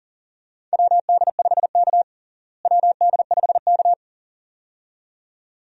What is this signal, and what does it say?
Morse code, heard as a keyed tone, about 40 wpm, 700 Hz. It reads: WD5K WD5K
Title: USA Call Signs - 4 Characters - Repeated 2x - 40wpm